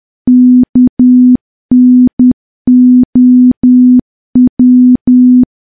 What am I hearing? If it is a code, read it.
Morse code, heard as a keyed tone, about 10 wpm, 250 Hz. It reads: KNOW
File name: morse / morse-training-10WPM